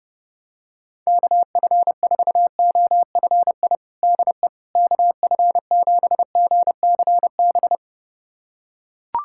Morse code, heard as a keyed tone, about 30 wpm, 700 Hz. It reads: KF4OFI DE KF7GCB
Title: USA Callsign DE Callsign - 6 Characters - 30wpm